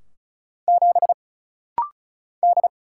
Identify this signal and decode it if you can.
Morse code, heard as a keyed tone, about 35 wpm, 700 Hz. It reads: 7 D
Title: Single Letter-Number - Rapid-Fire 35wpm